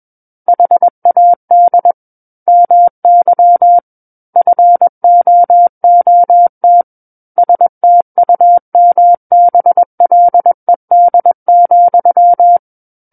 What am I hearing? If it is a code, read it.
Morse code, heard as a keyed tone, about 21 wpm, 700 Hz. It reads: HAD MY FOOT STUMBLED,
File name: Book One / 12-Chapter 12 - 21 WPM